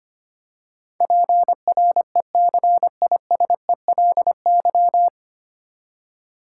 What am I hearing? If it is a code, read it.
Morse code, heard as a keyed tone, about 25 wpm, 700 Hz. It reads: PRECISELY
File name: Nine Letter Words - 25wpm